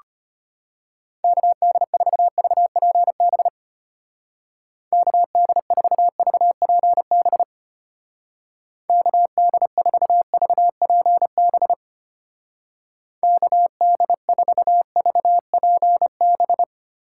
Morse code, heard as a keyed tone, about 31 wpm, 700 Hz. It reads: KD4VPB KD4VPB KD4VPB KD4VPB
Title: Speed Racing - USA 2x3 Call Signs 25wpm